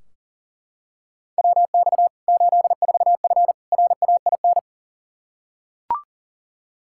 Morse code, heard as a keyed tone, about 40 wpm, 700 Hz. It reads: WX 84F RAIN
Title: K1USN SST QSO Practice 40wpm